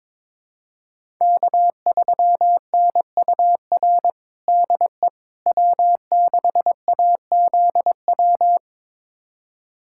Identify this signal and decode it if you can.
Morse code, heard as a keyed tone, about 22 wpm, 700 Hz. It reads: K3NUR DE W6AZW
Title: USA Callsign DE Callsign - 5 Characters - 22wpm